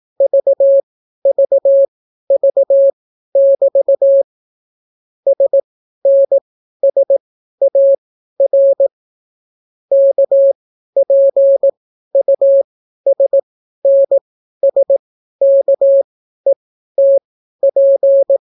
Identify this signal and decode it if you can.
Morse code, heard as a keyed tone, about 18 wpm, spaced as for 12 wpm, 550 Hz. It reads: VVV<BT> SNSAR KPUSNSKETP